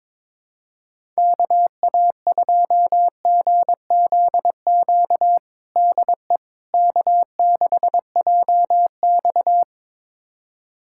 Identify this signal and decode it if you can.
Morse code, heard as a keyed tone, about 22 wpm, 700 Hz. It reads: KA2GZQ DE K6JX
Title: USA Callsign DE Callsign 22wpm